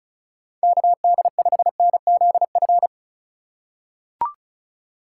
Morse code, heard as a keyed tone, about 36 wpm, 700 Hz. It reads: KD5NZF
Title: USA Call Signs - 6 Characters - 4x Character Spacing - 35wpm